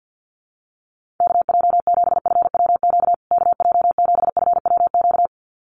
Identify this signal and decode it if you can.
Morse code, heard as a keyed tone, about 50 wpm, 700 Hz. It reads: KJ7FWQ KJ7FWQ
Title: USA Call Signs Repeated 2x 50wpm